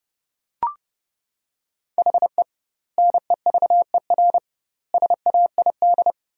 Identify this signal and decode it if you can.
Morse code, heard as a keyed tone, about 30 wpm, 700 Hz. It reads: HE NEVER SAID
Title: 2x Word-Spacing - Sentences from Top 500 Words - 30wpm